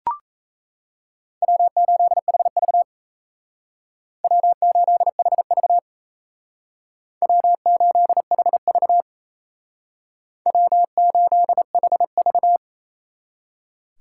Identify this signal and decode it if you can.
Morse code, heard as a keyed tone, about 36 wpm, 700 Hz. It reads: W8HV W8HV W8HV W8HV